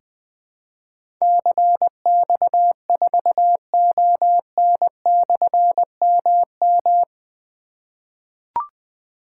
Morse code, heard as a keyed tone, about 20 wpm, 700 Hz. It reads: CX4ON/MM
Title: Difficult Call Signs 20wpm